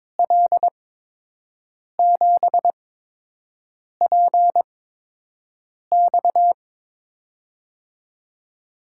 Morse code, heard as a keyed tone, about 22 wpm, spaced as for 6 wpm, 700 Hz. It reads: L7PX
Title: USA Call Signs - 8x Character Spacing - 22wpm